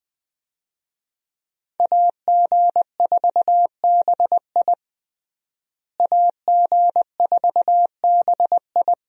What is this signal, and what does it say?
Morse code, heard as a keyed tone, about 20 wpm, 700 Hz. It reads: AG4BI AG4BI
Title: USA Call Signs - 5 Characters - Repeated 3x - 20wpm